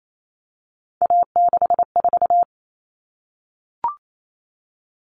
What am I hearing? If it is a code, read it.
Morse code, heard as a keyed tone, about 28 wpm, 700 Hz. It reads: A64
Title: All Cut Numbers - 3-Digit 28wpm